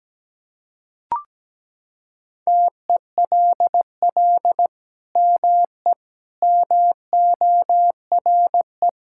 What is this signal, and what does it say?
Morse code, heard as a keyed tone, about 17 wpm, 700 Hz. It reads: TELL ME MORE
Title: Sentences from Top 1000 Words - Review Part 5 - 17wpm